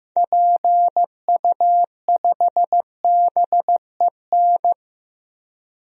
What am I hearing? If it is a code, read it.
Morse code, heard as a keyed tone, about 15 wpm, 700 Hz. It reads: PU5BEN
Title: DXCC-100 Call Signs - 6 Characters - Repeated 3x - 15wpm